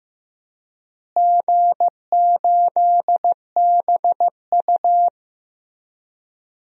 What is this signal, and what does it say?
Morse code, heard as a keyed tone, about 15 wpm, 700 Hz. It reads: G8BU
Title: DXCC-100 Call Signs - 4 Characters - Repeated 3x - 15wpm